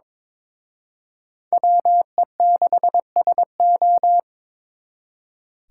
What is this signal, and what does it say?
Morse code, heard as a keyed tone, about 26 wpm, 700 Hz. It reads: WE6SO
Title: Speed Racing - USA Call Signs 22wpm